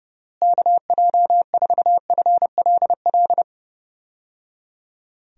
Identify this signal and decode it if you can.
Morse code, heard as a keyed tone, about 30 wpm, 700 Hz. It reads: KJ4FLL
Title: USA Call Signs - 6 Characters - Repeated 3x - 30wpm